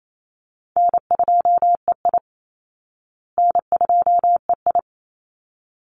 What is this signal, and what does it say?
Morse code, heard as a keyed tone, about 28 wpm, 700 Hz. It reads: N2EI N2EI
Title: Easy Call Signs Repeated 2x - 28wpm